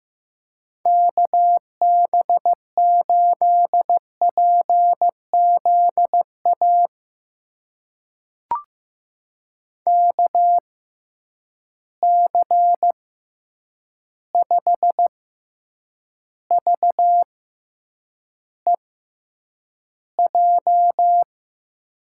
Morse code, heard as a keyed tone, about 15 wpm, spaced as for 8 wpm, 700 Hz. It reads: KB8PZA KC5VEJ
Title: USA Call Signs - 6 Characters - 6x Character Spacing - 15wpm